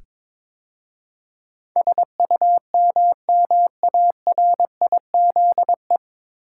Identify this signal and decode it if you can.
Morse code, heard as a keyed tone, about 21 wpm, 700 Hz. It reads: SUMMARIZE
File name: 9 Letter Words - First 3 Letters Given - 22wpm